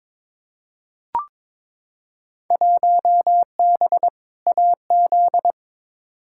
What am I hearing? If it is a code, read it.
Morse code, heard as a keyed tone, about 22 wpm, 700 Hz. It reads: 1B AZ
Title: Field Day Exchanges - 22wpm